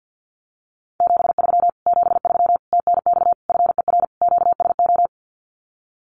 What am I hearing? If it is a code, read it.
Morse code, heard as a keyed tone, about 48 wpm, 700 Hz. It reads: 73 73 TNX FER QSO
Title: Common QSO Phrases Encore 50wpm